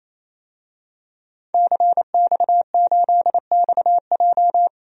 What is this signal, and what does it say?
Morse code, heard as a keyed tone, about 28 wpm, 700 Hz. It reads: CX8XJ
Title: DXCC-100 Call Signs - 5 Characters - 28wpm